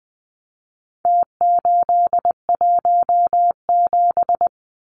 Morse code, heard as a keyed tone, about 20 wpm, 700 Hz. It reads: T817
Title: Numbers - Contest Style - 4-Digit 20wpm